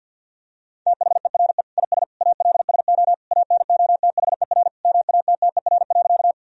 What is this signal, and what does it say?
Morse code, heard as a keyed tone, about 48 wpm, 700 Hz. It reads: THERE IS ALSO ANOTHER MATTER.
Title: Sentences from Top 1000 Words - Review Part 1 - 50wpm